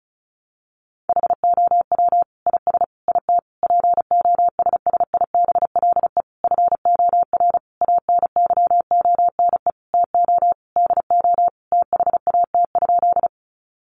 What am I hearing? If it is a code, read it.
Morse code, heard as a keyed tone, about 34 wpm, 700 Hz. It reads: HOW IS IT POSSIBLE FOR ANYONE TO DO THAT?
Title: Sentences from Top 900 Words 35wpm